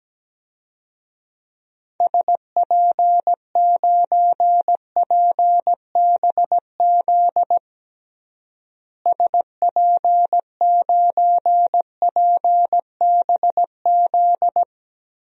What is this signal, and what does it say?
Morse code, heard as a keyed tone, about 17 wpm, 700 Hz. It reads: SP9PBZ SP9PBZ
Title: DXCC-100 Call Signs - 6 Characters - Repeated 2x - 17wpm